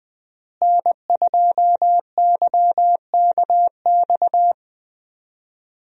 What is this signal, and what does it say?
Morse code, heard as a keyed tone, about 21 wpm, 700 Hz. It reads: N2YKX
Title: USA Call Signs - 5 Characters - Repeated 2x - 20wpm